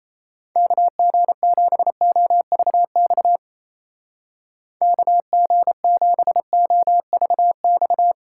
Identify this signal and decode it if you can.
Morse code, heard as a keyed tone, about 34 wpm, 700 Hz. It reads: KG7OVX KG7OVX
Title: Speed Racing - USA 2x3 Call Signs 28wpm